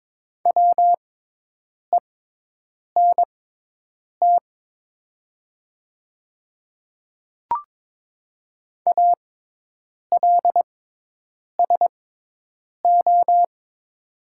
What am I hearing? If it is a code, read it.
Morse code, heard as a keyed tone, about 23 wpm, spaced as for 8 wpm, 700 Hz. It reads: WENT ALSO
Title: 6x Character Spacing - Top 200 Words - 22wpm